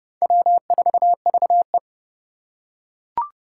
Morse code, heard as a keyed tone, about 30 wpm, 700 Hz. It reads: W4VE
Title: Four Challenging Letter-Number Combinations 30wpm